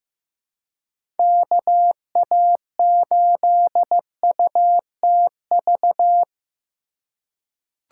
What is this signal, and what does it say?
Morse code, heard as a keyed tone, about 15 wpm, 700 Hz. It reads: KA8UTV
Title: USA Call Signs - 6 Characters - 2x Character Spacing - 15wpm